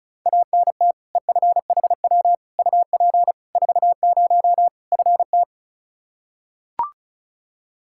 Morse code, heard as a keyed tone, about 36 wpm, 700 Hz. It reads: ANT EFHW UP 40 FT
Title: K1USN SST QSO Practice 35wpm